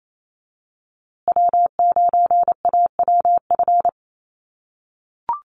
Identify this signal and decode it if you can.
Morse code, heard as a keyed tone, about 28 wpm, 700 Hz. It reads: W9AWF
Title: USA Call Signs Repeated 2x 28wpm